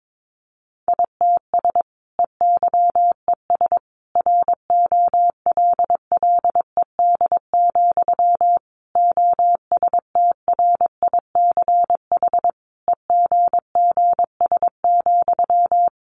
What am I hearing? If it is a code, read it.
Morse code, heard as a keyed tone, about 22 wpm, 700 Hz. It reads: ITS EYES ROLLED, OSTRICH EGGS,
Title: A Sound Of Thunder 22wpm